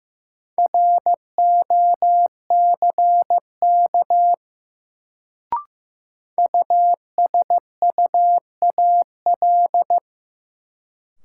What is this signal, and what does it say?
Morse code, heard as a keyed tone, about 15 wpm, 700 Hz. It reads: ROCK USUAL
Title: Sentences from Top 400 Words - New Word Drill - 15wpm